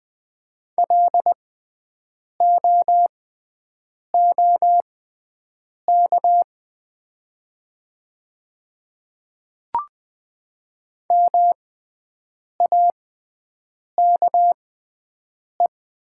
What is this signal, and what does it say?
Morse code, heard as a keyed tone, about 20 wpm, spaced as for 7 wpm, 700 Hz. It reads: LOOK MAKE
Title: 6x Character Spacing - Top 100 Words - 20wpm